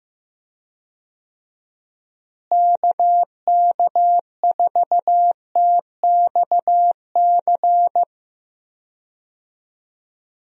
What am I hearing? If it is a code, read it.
Morse code, heard as a keyed tone, about 15 wpm, 700 Hz. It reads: KK4TXC
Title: USA Call Signs - 6 Characters - Repeated 3x - 15wpm